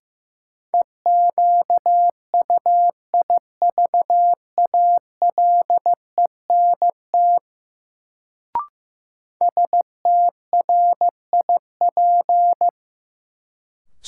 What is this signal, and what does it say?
Morse code, heard as a keyed tone, about 15 wpm, 700 Hz. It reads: EQUIVALENT STRIP
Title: Sentences from Top 1600 Words - New Word Drill Alt - 15wpm